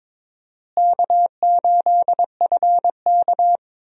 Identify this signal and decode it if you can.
Morse code, heard as a keyed tone, about 22 wpm, 700 Hz. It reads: K8FK
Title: USA Call Signs - 4 Characters - Repeated 2x - 22wpm